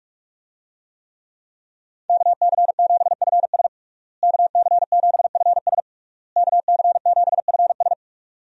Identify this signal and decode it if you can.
Morse code, heard as a keyed tone, about 46 wpm, 700 Hz. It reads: KC7FS KC7FS KC7FS
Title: USA Call Signs - 5 Characters - Repeated 3x - 45wpm